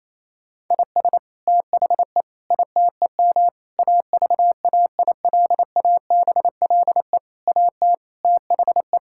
Morse code, heard as a keyed tone, about 28 wpm, 700 Hz. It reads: IS THE ITEM AVAILABLE AT THE
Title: Sentences from Top 1000 Words - Review Part 6 - 28wpm